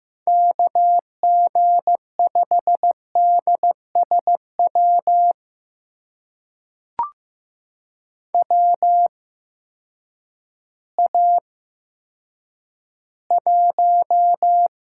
Morse code, heard as a keyed tone, about 15 wpm, spaced as for 9 wpm, 700 Hz. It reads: KG5DSW WA1
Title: USA Call Signs - 6 Characters - 8x Character Spacing - 15wpm